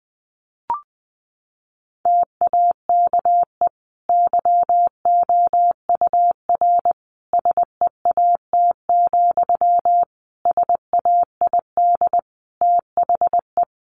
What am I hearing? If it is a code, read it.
Morse code, heard as a keyed tone, about 20 wpm, 700 Hz. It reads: TAKE YOUR SEAT, SAID THE
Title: Sentences from Top 1500 Words 20wpm